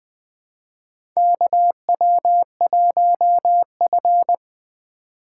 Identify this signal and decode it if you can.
Morse code, heard as a keyed tone, about 20 wpm, 700 Hz. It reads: KW1F
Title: USA Call Signs - 4 Characters - 4x Character Spacing - 20wpm